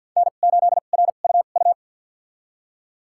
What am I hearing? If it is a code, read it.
Morse code, heard as a keyed tone, about 48 wpm, spaced as for 35 wpm, 700 Hz. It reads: N8RUU
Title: USA Call Signs - 5 Characters - 2x Character Spacing - 50wpm